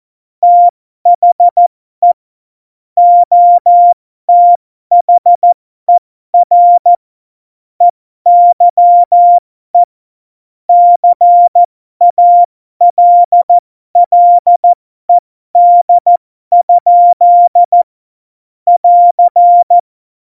Morse code, heard as a keyed tone, about 14 wpm, spaced as for 12 wpm, 700 Hz. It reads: THE OTHER EYE CALLED? <AR>
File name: Morsekurs / Witze